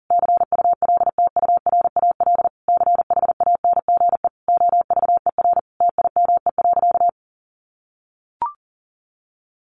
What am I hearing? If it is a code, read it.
Morse code, heard as a keyed tone, about 40 wpm, 700 Hz. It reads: CULTURAL CHANGE OVER TIME.
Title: Sentences from Top 1500 Words 40wpm